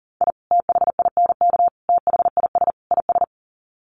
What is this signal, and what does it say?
Morse code, heard as a keyed tone, about 40 wpm, 700 Hz. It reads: I THINK THIS IS
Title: Sets of 4 Words 40wpm